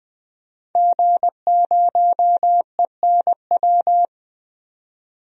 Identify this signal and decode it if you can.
Morse code, heard as a keyed tone, about 21 wpm, 700 Hz. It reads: G0ENW
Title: DXCC-100 Call Signs - 5 Characters - 20wpm